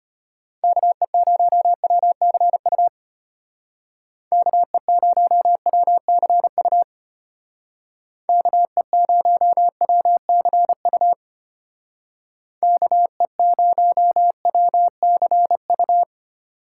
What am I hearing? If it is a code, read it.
Morse code, heard as a keyed tone, about 32 wpm, 700 Hz. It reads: KE0WCU KE0WCU KE0WCU KE0WCU